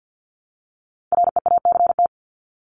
Morse code, heard as a keyed tone, about 50 wpm, 700 Hz. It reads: REACT
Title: Top 1251-1500 Words 50wpm